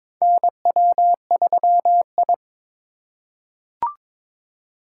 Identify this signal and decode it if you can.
Morse code, heard as a keyed tone, about 22 wpm, 700 Hz. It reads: NW3I
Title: USA Call Signs - 4 Characters - 8x Character Spacing - 22wpm